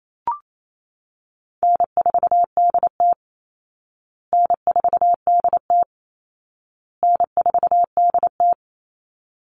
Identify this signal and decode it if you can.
Morse code, heard as a keyed tone, about 28 wpm, 700 Hz. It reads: N4DT N4DT N4DT
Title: Call Signs Repeated 2x - 28wpm